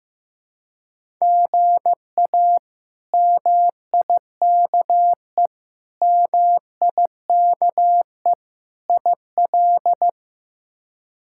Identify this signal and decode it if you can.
Morse code, heard as a keyed tone, about 15 wpm, 700 Hz. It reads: GA MIKE MIKE IL
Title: K1USN SST QSO Practice - 15wpm